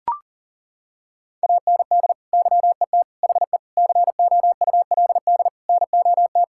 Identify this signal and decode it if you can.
Morse code, heard as a keyed tone, about 40 wpm, 700 Hz. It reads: AND YET HE COULD NOT
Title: Sentences from Top 1600 Words 40wpm